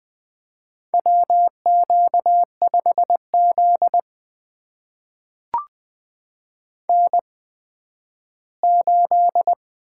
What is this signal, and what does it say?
Morse code, heard as a keyed tone, about 20 wpm, spaced as for 12 wpm, 700 Hz. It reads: WQ5Z N8
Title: USA Call Signs - 8x Character Spacing - 20wpm